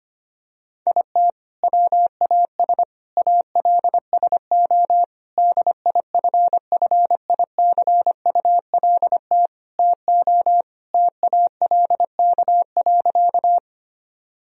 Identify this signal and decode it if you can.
Morse code, heard as a keyed tone, about 25 wpm, 700 Hz. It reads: IT WAS ALSO DIFFICULT TO TALK.
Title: Sentences from Top 1000 Words - Review Part 1 - 25wpm